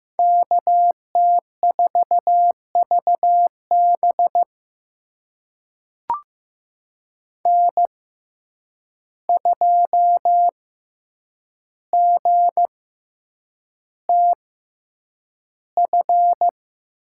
Morse code, heard as a keyed tone, about 15 wpm, spaced as for 8 wpm, 700 Hz. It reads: KT4VB N2GTF
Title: USA Call Signs - 5 Characters - 6x Character Spacing - 15wpm